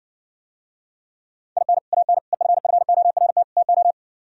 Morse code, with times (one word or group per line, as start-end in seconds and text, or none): 1.6-1.8 IN
1.9-2.2 AN
2.3-3.4 EFFORT
3.6-3.9 TO